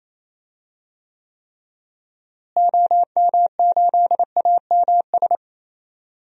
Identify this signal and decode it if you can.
Morse code, heard as a keyed tone, about 28 wpm, 700 Hz. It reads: OM8AMS
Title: DXCC-100 Call Signs - 6 Characters - Repeated 3x - 28wpm